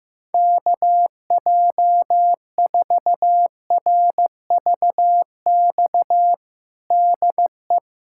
Morse code, heard as a keyed tone, about 15 wpm, 700 Hz. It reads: KJ4RVX DE
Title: USA Callsign DE Callsign - 6 Characters - 15wpm